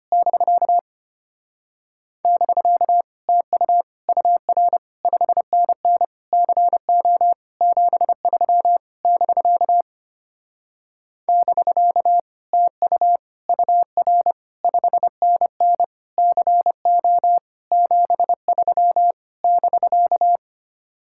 Morse code, heard as a keyed tone, about 28 wpm, 700 Hz. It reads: <BK> <BK> TU UR 5NN CO 73 <BK> <BK> TU UR 5NN CO 73 <BK>
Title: Speed-Racing Common QSO Phrases - 25wpm